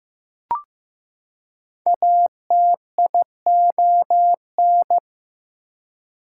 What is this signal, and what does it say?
Morse code, heard as a keyed tone, about 15 wpm, 700 Hz. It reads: ATION
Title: Prefixes and Suffixes 15wpm